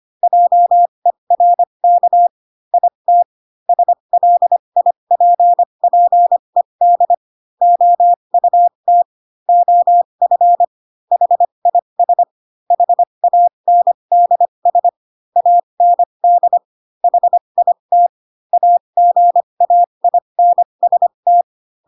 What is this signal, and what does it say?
Morse code, heard as a keyed tone, about 25 wpm, spaced as for 22 wpm, 700 Hz. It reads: JERK IT SLIPPED OUT OF HIS HANDS AND HIT AGAINST
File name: Pinocchio / Pinocchio ch2 18 25 ik1hge